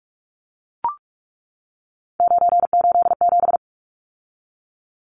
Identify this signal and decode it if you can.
Morse code, heard as a keyed tone, about 44 wpm, 700 Hz. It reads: 987
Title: Numbers - 3-Digit 45wpm